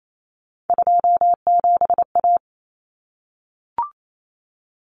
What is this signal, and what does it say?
Morse code, heard as a keyed tone, about 28 wpm, 700 Hz. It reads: 27A
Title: Common Cut Numbers - 3-Digit 28wpm